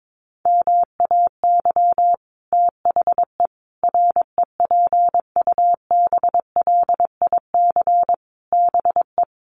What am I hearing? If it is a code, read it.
Morse code, heard as a keyed tone, about 22 wpm, 700 Hz. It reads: MAY THE REPUBLIC BE
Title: Sentences from Top 1500 Words 22wpm